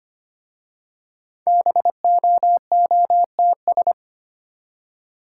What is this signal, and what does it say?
Morse code, heard as a keyed tone, about 25 wpm, 700 Hz. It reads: BOOTS
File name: practice / 25wpm / Easy Words with Hints - 25wpm